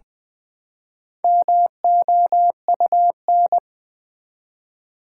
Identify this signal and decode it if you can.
Morse code, heard as a keyed tone, about 20 wpm, 700 Hz. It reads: MOUN
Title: Build A Word - 1 Letter At A Time - 5-Letter Words - 20wpm